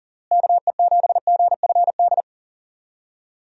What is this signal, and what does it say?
Morse code, heard as a keyed tone, about 40 wpm, 700 Hz. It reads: KE7GFD
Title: USA Call Signs - 6 Characters - Repeated 3x - 40wpm